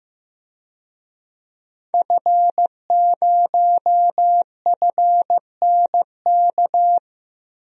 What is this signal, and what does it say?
Morse code, heard as a keyed tone, about 15 wpm, 700 Hz. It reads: F0FNK
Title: DXCC-100 Call Signs - 5 Characters - 15wpm